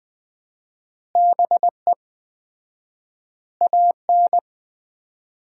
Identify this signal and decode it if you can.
Morse code, heard as a keyed tone, about 20 wpm, 700 Hz. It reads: BE AN
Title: 4x Word-Spacing - Sets of 3 Words - 20wpm